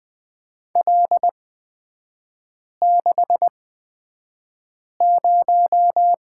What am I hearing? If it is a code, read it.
Morse code, heard as a keyed tone, about 20 wpm, spaced as for 5 wpm, 700 Hz. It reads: L60
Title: 243 Lesson 39e - Letter z - Three Characters 20wpm